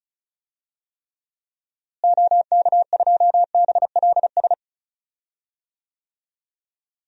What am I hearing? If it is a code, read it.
Morse code, heard as a keyed tone, about 34 wpm, 700 Hz. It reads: OK2BLS
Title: DXCC-100 Call Signs - 6 Characters - Repeated 2x - 35wpm